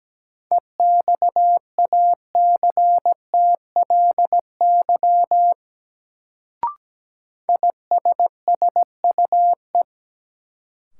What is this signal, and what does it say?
Morse code, heard as a keyed tone, about 17 wpm, 700 Hz. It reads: EXACTLY ISSUE
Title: Sentences from Top 900 Words - New Word Drill - 17wpm